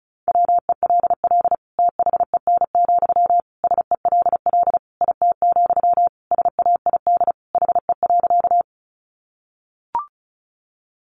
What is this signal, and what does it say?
Morse code, heard as a keyed tone, about 34 wpm, 700 Hz. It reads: WELL THEN, SELL IT, SAID HE.
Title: Sentences from Top 1100 Words 35wpm